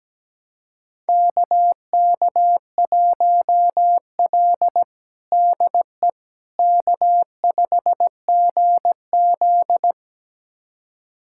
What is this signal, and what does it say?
Morse code, heard as a keyed tone, about 17 wpm, 700 Hz. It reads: KK1L DE K5GZ
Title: USA Callsign DE Callsign - 4 Characters - 17wpm